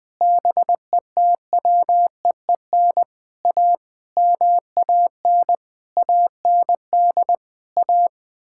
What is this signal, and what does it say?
Morse code, heard as a keyed tone, about 20 wpm, 700 Hz. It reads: BETWEEN A MAN AND A
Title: Sentences with 6-Word Phrases - 20wpm